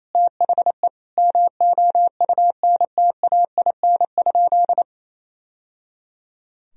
Morse code, heard as a keyed tone, about 28 wpm, 700 Hz. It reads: THE MOUNTAIN?
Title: Sentences from Top 1000 Words - Review Part 1 - 28wpm